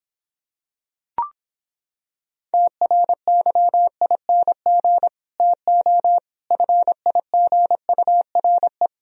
Morse code, heard as a keyed tone, about 26 wpm, 700 Hz. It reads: TRYING TO FIGURE